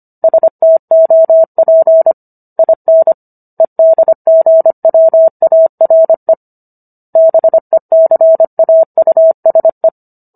Morse code, heard as a keyed tone, about 25 wpm, 650 Hz. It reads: STOP IN EDGWARE BECAUSE
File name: Book One / 16-Chapter 16 - 25 WPM